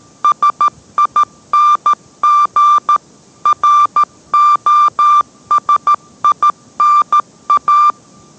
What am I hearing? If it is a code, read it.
Morse code, heard as a keyed tone, about 13 wpm, 1180 Hz. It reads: SING ROSINA